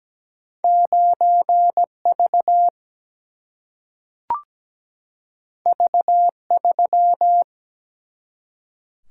Morse code, heard as a keyed tone, about 17 wpm, 700 Hz. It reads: 9 V V 3
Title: Single-Letter Single-Number 17wpm